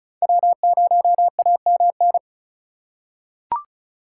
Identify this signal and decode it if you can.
Morse code, heard as a keyed tone, about 36 wpm, 700 Hz. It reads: W0AMN
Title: USA Call Signs - 5 Characters - 4x Character Spacing - 35wpm